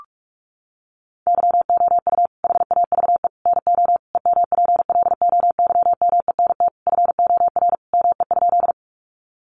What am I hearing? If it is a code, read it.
Morse code, heard as a keyed tone, about 44 wpm, 700 Hz. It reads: YOU HAVE NO EMPLOYMENT FOR ME?
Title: Sentences from Top 1500 Words 45wpm